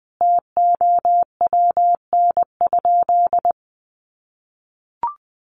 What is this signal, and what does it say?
Morse code, heard as a keyed tone, about 19 wpm, 700 Hz. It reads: TOWN?
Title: Sentences from Top 1300 Words 20wpm